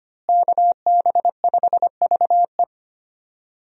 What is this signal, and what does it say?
Morse code, heard as a keyed tone, about 25 wpm, 700 Hz. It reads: KB5VE
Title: USA Call Signs - 5 Characters - Repeated 3x - 25wpm